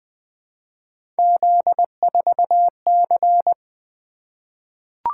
Morse code, heard as a keyed tone, about 20 wpm, 700 Hz. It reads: Z4C
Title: Three Letter-Number Combinations - 20wpm